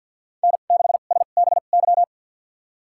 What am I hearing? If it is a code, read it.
Morse code, heard as a keyed tone, about 51 wpm, spaced as for 36 wpm, 700 Hz. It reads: N6SBY
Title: USA Call Signs - 5 Characters - 2x Character Spacing - 50wpm